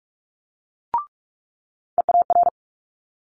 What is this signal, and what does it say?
Morse code, heard as a keyed tone, about 46 wpm, 700 Hz. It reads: EAR
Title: Multi-letter Phonemes Quick Fire 45wpm